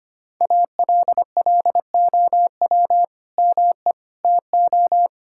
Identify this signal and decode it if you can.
Morse code, heard as a keyed tone, about 25 wpm, 700 Hz. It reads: ALLOW ME TO
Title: Sentences from Top 1000 Words - Review Part 5 - 25wpm